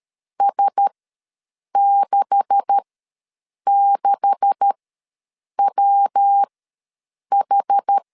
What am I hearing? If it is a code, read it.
Morse code, heard as a keyed tone, about 13 wpm, spaced as for 7 wpm, 800 Hz. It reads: S66WH
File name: LEZIONI / 9 286 ws 12wpm 3351